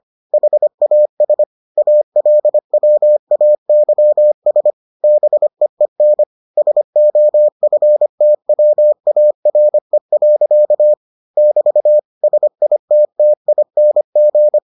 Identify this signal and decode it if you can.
Morse code, heard as a keyed tone, about 25 wpm, 600 Hz. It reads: HAS ALWAYS BEEN SOFTWARE. <BT> SITTING